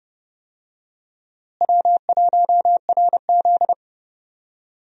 Morse code, heard as a keyed tone, about 30 wpm, 700 Hz. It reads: W1RZ
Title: USA Call Signs - 4 Characters - Repeated 2x - 30wpm